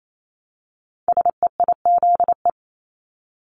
Morse code, heard as a keyed tone, about 28 wpm, 700 Hz. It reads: SEIZE
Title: Multi-letter Phonemes Repeated 3x 28wpm